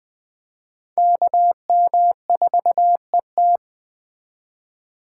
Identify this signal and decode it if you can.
Morse code, heard as a keyed tone, about 21 wpm, 700 Hz. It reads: KM4ET
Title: Speed Racing - USA 2x2 Call Signs 17wpm